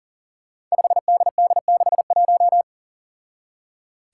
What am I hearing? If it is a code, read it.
Morse code, heard as a keyed tone, about 40 wpm, 700 Hz. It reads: 5DD61
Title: Five Challenging Letter-Number Combinations 40wpm